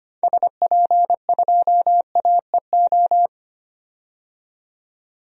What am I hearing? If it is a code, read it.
Morse code, heard as a keyed tone, about 25 wpm, 700 Hz. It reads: SP2AEO
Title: DXCC-100 Call Signs - 6 Characters - 25wpm